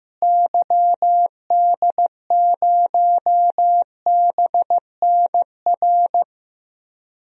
Morse code, heard as a keyed tone, about 15 wpm, 700 Hz. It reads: YD0BNR
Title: DXCC-100 Call Signs - 6 Characters - Repeated 3x - 15wpm